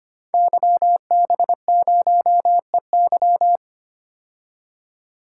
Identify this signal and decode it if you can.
Morse code, heard as a keyed tone, about 25 wpm, 700 Hz. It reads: YB0EY
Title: Code Groups - 5 Letter:Numbers - 25wpm